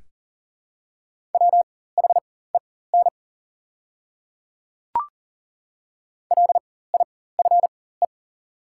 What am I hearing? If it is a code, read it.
Morse code, heard as a keyed tone, about 40 wpm, spaced as for 19 wpm, 690 Hz. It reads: WHEN LIFE
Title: 4x Character Spacing - Top 500 Words - 40wpm